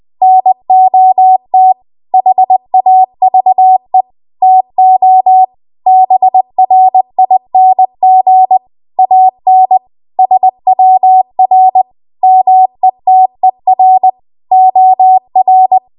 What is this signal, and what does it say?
Morse code, heard as a keyed tone, about 20 wpm, 750 Hz. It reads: NOT HAVE TO BRING AN SWR METER OR
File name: Archive / 20 WPM / 20WPM